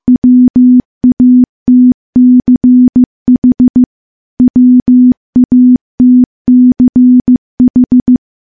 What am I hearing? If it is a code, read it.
Morse code, heard as a keyed tone, about 15 wpm, 250 Hz. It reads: WATCH WATCH
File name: morse-training-15WPM